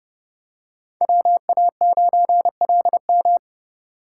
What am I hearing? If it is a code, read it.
Morse code, heard as a keyed tone, about 28 wpm, 700 Hz. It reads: WA9LM